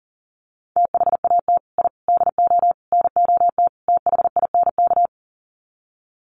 Morse code, heard as a keyed tone, about 40 wpm, 700 Hz. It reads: THAT I DO NOT THINK
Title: Sets of 5 Words Encore 40wpm